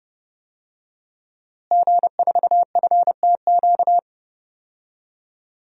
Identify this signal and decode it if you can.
Morse code, heard as a keyed tone, about 30 wpm, 700 Hz. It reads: G4FTQ
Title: DXCC-100 Call Signs - 5 Characters - Repeated 2x - 30wpm